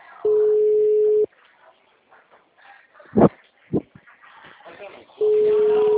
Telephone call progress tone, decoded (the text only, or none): ringback tone